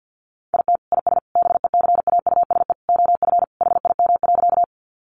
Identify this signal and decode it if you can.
Morse code, heard as a keyed tone, about 50 wpm, 700 Hz. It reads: IT IS BECAUSE OF HIM.
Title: Sentences from Top 600 Words 50wpm